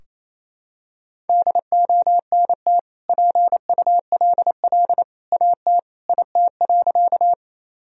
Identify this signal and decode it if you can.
Morse code, heard as a keyed tone, about 28 wpm, 700 Hz. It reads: DONT PULL AT IT.
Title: Sentences from Top 1000 Words - Review Part 2 - 28wpm